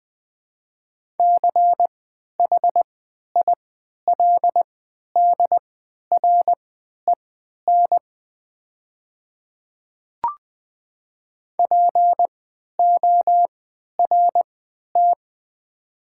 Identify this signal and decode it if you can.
Morse code, heard as a keyed tone, about 20 wpm, spaced as for 11 wpm, 700 Hz. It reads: CHILDREN PORT
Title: 3x Character Spacing - Top 300 Words - 20wpm